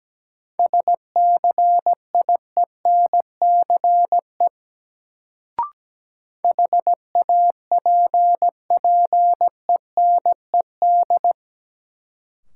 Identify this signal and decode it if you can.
Morse code, heard as a keyed tone, about 17 wpm, 700 Hz. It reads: SCIENCE HAPPENED
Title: Sentences from Top 1000 Words - Review Part 2 - New Word Drill - 17wpm